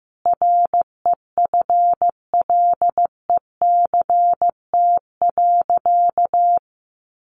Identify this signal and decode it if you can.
Morse code, heard as a keyed tone, about 15 wpm, 700 Hz. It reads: REFLECT.
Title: Sentences from Top 1500 Words 15wpm